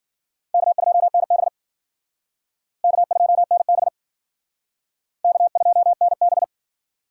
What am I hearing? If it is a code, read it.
Morse code, heard as a keyed tone, about 52 wpm, 700 Hz. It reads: K2NB K2NB K2NB